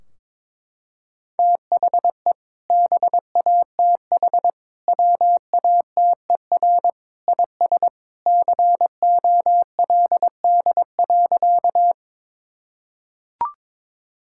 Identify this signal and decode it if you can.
Morse code, heard as a keyed tone, about 22 wpm, 700 Hz. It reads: THE BATH WATER IS COLD.
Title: Sentences from Top 1600 Words 22wpm